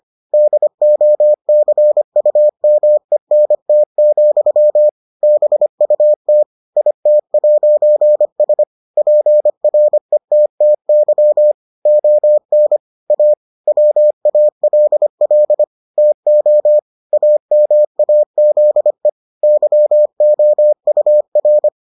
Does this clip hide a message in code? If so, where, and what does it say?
Yes: Morse code from 0.3 to 21.7 s, heard as a keyed tone, about 25 wpm, 600 Hz. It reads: DOCUMENT, BUT IT'S PRETTY ON A WALL TO AMAZE YOUR